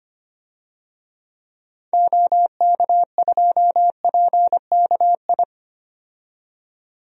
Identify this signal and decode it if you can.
Morse code, heard as a keyed tone, about 25 wpm, 700 Hz. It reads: OK2PKI
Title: DXCC-100 Call Signs - 6 Characters - 25wpm